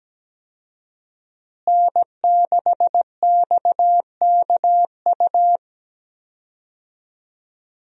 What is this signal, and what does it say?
Morse code, heard as a keyed tone, about 17 wpm, 700 Hz. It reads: N6XKU
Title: USA Call Signs - 5 Characters - Repeated 3x - 17wpm